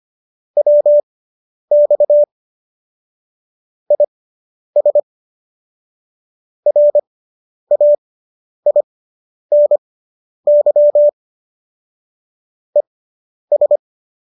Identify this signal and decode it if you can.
Morse code, heard as a keyed tone, about 25 wpm, spaced as for 10 wpm, 590 Hz. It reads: WX IS RAINY ES